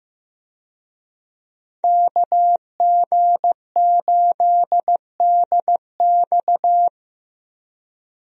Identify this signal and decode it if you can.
Morse code, heard as a keyed tone, about 15 wpm, 700 Hz. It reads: KG8DX
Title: USA Call Signs - 5 Characters - Repeated 2x - 15wpm